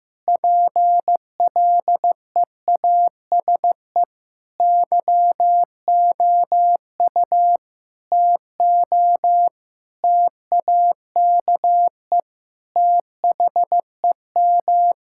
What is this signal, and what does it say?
Morse code, heard as a keyed tone, about 15 wpm, 700 Hz. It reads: PLEASE YOU TO TAKE THEM
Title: Sentences from Top 1600 Words 15wpm